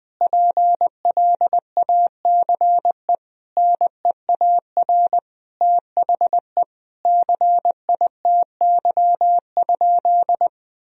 Morse code, heard as a keyed tone, about 21 wpm, 700 Hz. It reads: PLACE NEAR THE CITY?
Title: Sentences from Top 1000 Words - Review Part 3 - 20wpm